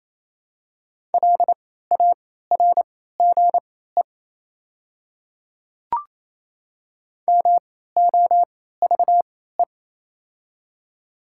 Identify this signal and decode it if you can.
Morse code, heard as a keyed tone, about 28 wpm, spaced as for 16 wpm, 690 Hz. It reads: LARGE MOVE
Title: 3x Character Spacing - Top 300 Words - 28wpm